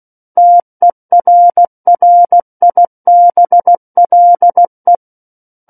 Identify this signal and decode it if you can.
Morse code, heard as a keyed tone, about 16 wpm, 700 Hz. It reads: TERRIBLE
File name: Book One / 07-Chapter 7 - 16 WPM